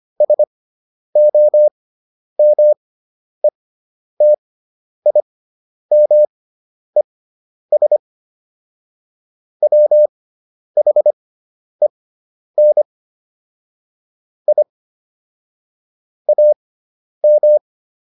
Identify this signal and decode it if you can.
Morse code, heard as a keyed tone, about 25 wpm, spaced as for 10 wpm, 600 Hz. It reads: SOMETIMES WHEN I AM